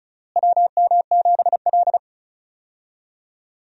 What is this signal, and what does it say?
Morse code, heard as a keyed tone, about 36 wpm, 700 Hz. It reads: WM7L
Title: USA Call Signs - 4 Characters - Repeated 3x - 35wpm